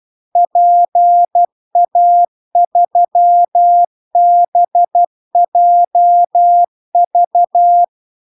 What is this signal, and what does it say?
Morse code, heard as a keyed tone, about 12 wpm, 700 Hz. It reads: PA3BJV
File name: cw-cursus-VERON-a03